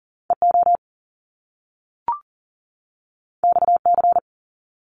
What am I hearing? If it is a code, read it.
Morse code, heard as a keyed tone, about 42 wpm, 700 Hz. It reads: EO XC
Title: Top Two Letter Combinations in Words 40wpm